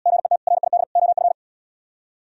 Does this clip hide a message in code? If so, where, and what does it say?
Yes: Morse code from 0.1 to 1.3 s, heard as a keyed tone, about 76 wpm, 700 Hz. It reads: GET RID OF